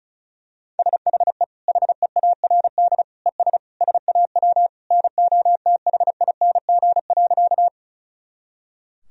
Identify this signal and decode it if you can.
Morse code, heard as a keyed tone, about 36 wpm, 700 Hz. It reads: SHE HEARD ES SAW NOTHING.
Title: Sentences from Top 500 Words and Abbreviations 35wpm